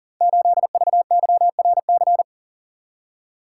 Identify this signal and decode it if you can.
Morse code, heard as a keyed tone, about 38 wpm, 700 Hz. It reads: 8VYRC
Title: Code Groups - 5 Letter:Numbers - 40wpm